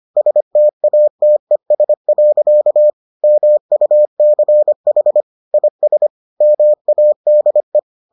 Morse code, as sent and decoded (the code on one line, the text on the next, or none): ... - .- - . ... .-.-.- / -- ..- -.-. .... / .. ... / -- .- -.. .
STATES. MUCH IS MADE